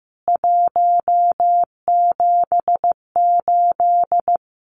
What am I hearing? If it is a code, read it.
Morse code, heard as a keyed tone, about 15 wpm, 700 Hz. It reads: 178
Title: CWT Exchange 15wpm